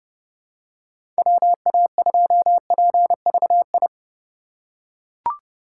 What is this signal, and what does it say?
Morse code, heard as a keyed tone, about 30 wpm, 700 Hz. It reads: WA2PVI
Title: USA Call Signs - 6 Characters - 4x Character Spacing - 30wpm